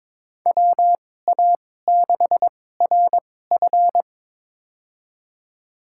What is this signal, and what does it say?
Morse code, heard as a keyed tone, about 21 wpm, spaced as for 16 wpm, 700 Hz. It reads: WA6RF
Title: USA Call Signs - 2x Character Spacing - 22wpm